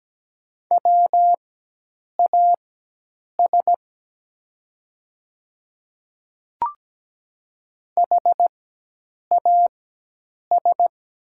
Morse code, heard as a keyed tone, about 17 wpm, spaced as for 8 wpm, 700 Hz. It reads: WAS HAS
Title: 4x Character Spacing - Top 100 Words - 17wpm